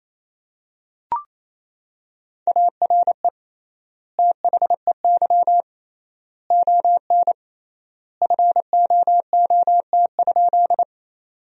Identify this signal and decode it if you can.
Morse code, heard as a keyed tone, about 27 wpm, 700 Hz. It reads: ARE THEY ON FOOT?
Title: 3x Word-Spacing - Sentences from Top 500 Words - 28wpm